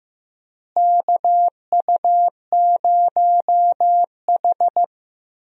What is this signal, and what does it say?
Morse code, heard as a keyed tone, about 15 wpm, 700 Hz. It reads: KU0H
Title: USA Call Signs - 4 Characters - Repeated 3x - 15wpm